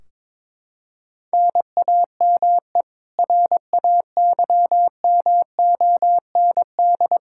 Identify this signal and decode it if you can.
Morse code, heard as a keyed tone, about 21 wpm, 700 Hz. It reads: NAME RAYMOND
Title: Practice QSO - Exchange 1 - Beginning - 22wpm